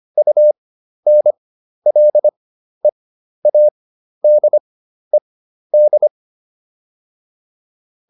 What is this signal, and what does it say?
Morse code, heard as a keyed tone, about 25 wpm, spaced as for 12 wpm, 600 Hz. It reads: UNLEADED